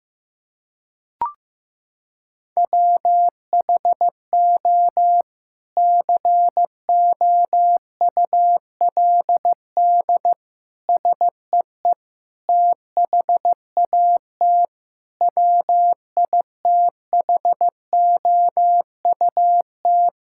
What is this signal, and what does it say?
Morse code, heard as a keyed tone, about 15 wpm, 700 Hz. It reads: WHO COULD SEE THAT WITHOUT
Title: Sentences from Top 1600 Words 15wpm